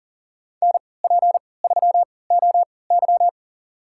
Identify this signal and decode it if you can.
Morse code, heard as a keyed tone, about 40 wpm, spaced as for 23 wpm, 700 Hz. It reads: NP3OY
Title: USA Call Signs - 5 Characters - 3x Character Spacing - 40wpm